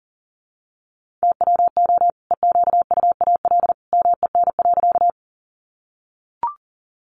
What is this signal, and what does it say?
Morse code, heard as a keyed tone, about 40 wpm, 700 Hz. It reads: TWO EQUAL MEN.
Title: Sentences from Top 1400 Words 40wpm